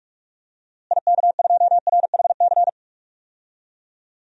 Morse code, heard as a keyed tone, about 46 wpm, 700 Hz. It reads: IK2RHC
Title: DXCC-100 Call Signs - 6 Characters - Repeated 2x - 45wpm